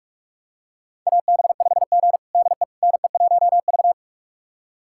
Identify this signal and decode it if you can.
Morse code, heard as a keyed tone, about 42 wpm, 700 Hz. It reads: AB5G DE NE1V
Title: USA Callsign DE Callsign - 4 Characters - 45wpm